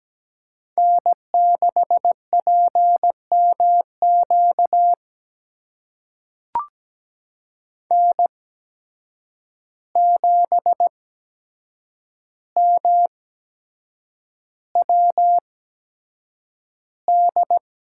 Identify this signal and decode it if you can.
Morse code, heard as a keyed tone, about 17 wpm, spaced as for 7 wpm, 700 Hz. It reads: N6PMQ N7MWD
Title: USA Call Signs - 5 Characters - 8x Character Spacing - 17wpm